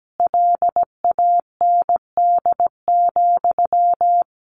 Morse code, heard as a keyed tone, about 17 wpm, 700 Hz. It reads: LAND,
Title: Sentences from Top 1500 Words 17wpm